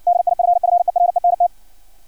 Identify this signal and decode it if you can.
Morse code, heard as a keyed tone, about 56 wpm, 700 Hz. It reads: DIFFERENT
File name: archived / CRSnet